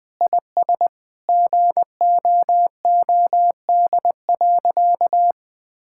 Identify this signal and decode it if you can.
Morse code, heard as a keyed tone, about 20 wpm, 700 Hz. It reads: IS GOOD.
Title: Sentences with 3-Word Phrases - 20wpm